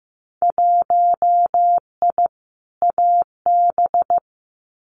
Morse code, heard as a keyed tone, about 15 wpm, 700 Hz. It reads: 1I AB
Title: Winter Field Day Exchanges Encore 15wpm